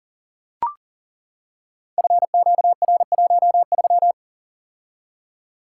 Morse code, heard as a keyed tone, about 42 wpm, 700 Hz. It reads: FQR13
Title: Code Groups - 5 Letter:Numbers - 40wpm